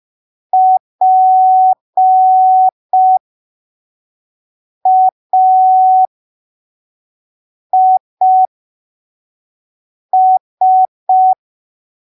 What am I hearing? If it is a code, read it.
Morse code, heard as a keyed tone, about 5 wpm, spaced as for 3 wpm, 750 Hz. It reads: PAIS
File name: arquivos / texto / texto 22 05ppm